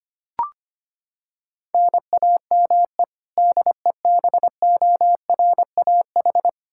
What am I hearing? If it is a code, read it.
Morse code, heard as a keyed tone, about 25 wpm, 700 Hz. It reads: NAME DEBORAH
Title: Practice QSO - Exchange 1 - Beginning - 25wpm